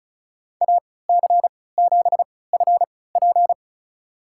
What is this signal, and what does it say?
Morse code, heard as a keyed tone, about 34 wpm, spaced as for 20 wpm, 700 Hz. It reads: AC7FP
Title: USA Call Signs - 5 Characters - 3x Character Spacing - 35wpm